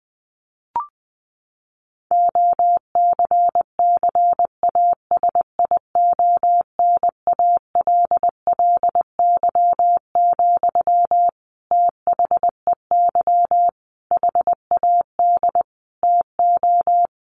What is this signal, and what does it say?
Morse code, heard as a keyed tone, about 20 wpm, 700 Hz. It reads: OCCASIONALLY, THEY HAD TO
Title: Sentences from Top 1400 Words 20wpm